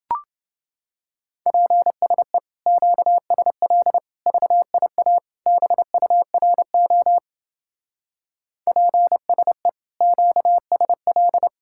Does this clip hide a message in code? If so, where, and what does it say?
Yes: Morse code from 1.5 to 11.5 s, heard as a keyed tone, about 29 wpm, 700 Hz. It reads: PSE QSL VIA BURO PSE QSL